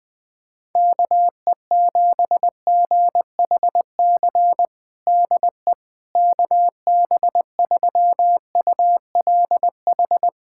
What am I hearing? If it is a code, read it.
Morse code, heard as a keyed tone, about 21 wpm, 700 Hz. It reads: KE7GHC DE KB3ULH
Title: USA Callsign DE Callsign - 6 Characters - 20wpm